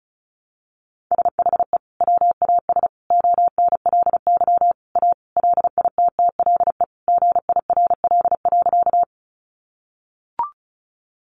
Morse code, heard as a keyed tone, about 34 wpm, 700 Hz. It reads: SHE WAS ONLY A LITTLE GIRL.
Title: Sentences from Top 400 Words 35wpm